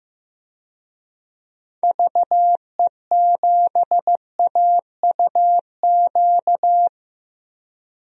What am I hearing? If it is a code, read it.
Morse code, heard as a keyed tone, about 15 wpm, 700 Hz. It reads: VE7AUQ
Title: DXCC-100 Call Signs - 6 Characters - Repeated 3x - 15wpm